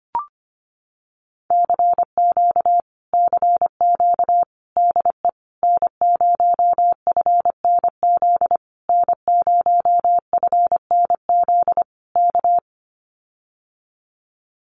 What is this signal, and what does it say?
Morse code, heard as a keyed tone, about 25 wpm, 700 Hz. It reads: CQ CQ DE N0FNZ N0FNZ K
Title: Common QSO Phrases Encore 25wpm